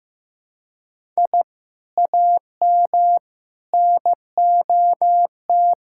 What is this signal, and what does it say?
Morse code, heard as a keyed tone, about 15 wpm, 700 Hz. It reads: I AM NOT
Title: Sentences from Top 1000 Words - Review Part 5 - 15wpm